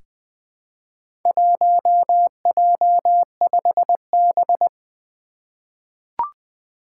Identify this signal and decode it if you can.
Morse code, heard as a keyed tone, about 20 wpm, 700 Hz. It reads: 1J5B
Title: Four Challenging Letter-Number Combinations 20wpm